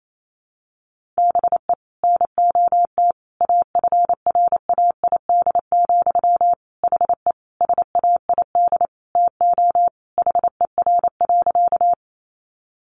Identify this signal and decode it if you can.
Morse code, heard as a keyed tone, about 28 wpm, 700 Hz. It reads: BE NOT AFRAID, HE SAID TO HER.
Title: Sentences from Top 1500 Words 28wpm